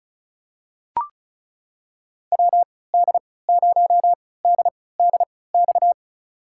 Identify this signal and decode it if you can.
Morse code, heard as a keyed tone, about 36 wpm, spaced as for 20 wpm, 700 Hz. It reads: WD0DDX
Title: USA Call Signs - 3x Character Spacing - 35wpm